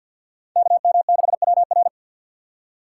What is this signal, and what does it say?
Morse code, heard as a keyed tone, about 52 wpm, 690 Hz. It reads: KM6WR
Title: USA Call Signs - 5 Characters - 4x Character Spacing - 50wpm